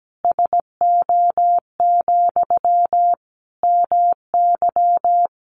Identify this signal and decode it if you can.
Morse code, heard as a keyed tone, about 17 wpm, 700 Hz. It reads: SO, MY
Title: Sentences from Top 1200 Words 17wpm